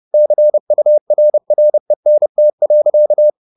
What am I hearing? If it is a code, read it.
Morse code, heard as a keyed tone, about 30 wpm, 600 Hz. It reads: CURRENT.